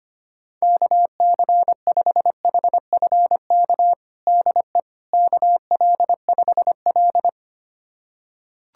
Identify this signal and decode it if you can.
Morse code, heard as a keyed tone, about 25 wpm, 700 Hz. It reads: KC5HFK DE KL5L
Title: USA Callsign DE Callsign 25wpm